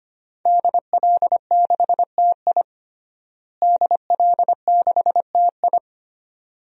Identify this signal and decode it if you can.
Morse code, heard as a keyed tone, about 25 wpm, 700 Hz. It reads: DL6TI DL6TI
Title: DXCC-100 Call Signs - 5 Characters - Repeated 2x - 25wpm